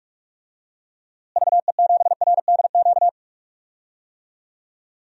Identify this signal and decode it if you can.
Morse code, heard as a keyed tone, about 47 wpm, 700 Hz. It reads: VE7RDQ
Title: DXCC-100 Call Signs - 6 Characters - Repeated 3x - 45wpm